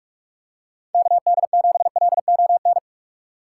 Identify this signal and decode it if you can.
Morse code, heard as a keyed tone, about 46 wpm, 700 Hz. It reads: KD7LON
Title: USA Call Signs - 6 Characters - 3x Character Spacing - 45wpm